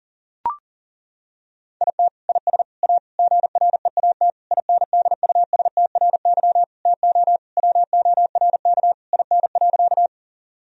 Morse code, heard as a keyed tone, about 40 wpm, 700 Hz. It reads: IT IS A GREAT INDUSTRY TO WORK IN.
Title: Sentences from Top 1000 Words - Review Part 4 - 40wpm